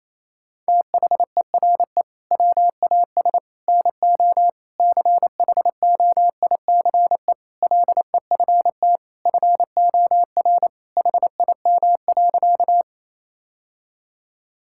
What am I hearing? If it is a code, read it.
Morse code, heard as a keyed tone, about 27 wpm, 700 Hz. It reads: THERE WAS NO CHOICE LEFT FOR HIM.
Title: Sentences from Top 1000 Words - Review Part 6 - 28wpm